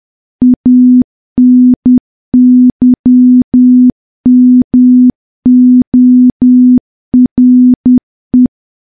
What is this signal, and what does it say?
Morse code, heard as a keyed tone, about 10 wpm, 250 Hz. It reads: ANYMORE
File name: morse / morse-training-10WPM